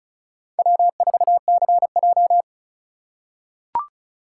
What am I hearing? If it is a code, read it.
Morse code, heard as a keyed tone, about 36 wpm, 700 Hz. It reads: W4CJ
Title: USA Call Signs - 4 Characters - 4x Character Spacing - 35wpm